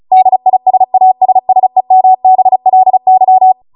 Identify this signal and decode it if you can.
Morse code, heard as a keyed tone, about 36 wpm, 750 Hz. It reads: DISASSEMBLY